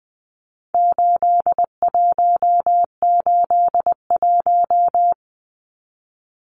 Morse code, heard as a keyed tone, about 20 wpm, 700 Hz. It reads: 8181
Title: Numbers - 4-Digit 20wpm